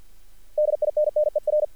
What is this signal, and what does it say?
Morse code, heard as a keyed tone, about 48 wpm, 600 Hz. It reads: DINNER
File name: archived / CRSnet